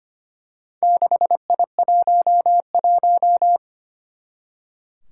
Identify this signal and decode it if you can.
Morse code, heard as a keyed tone, about 25 wpm, 700 Hz. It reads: 6I11
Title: Four Challenging Letter-Number Combinations 25wpm